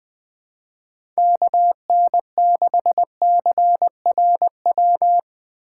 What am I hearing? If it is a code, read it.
Morse code, heard as a keyed tone, about 20 wpm, 700 Hz. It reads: KN6CRW
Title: USA Call Signs - 2x Character Spacing - 20wpm